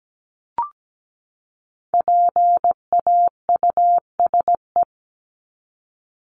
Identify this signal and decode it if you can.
Morse code, heard as a keyed tone, about 17 wpm, 700 Hz. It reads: PAUSE
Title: Top 1501-1750 Words 17wpm